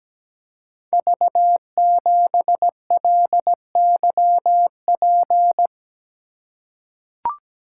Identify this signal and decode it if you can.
Morse code, heard as a keyed tone, about 17 wpm, 700 Hz. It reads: V7LYP
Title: Code Groups - 5 Letter:Numbers - 17wpm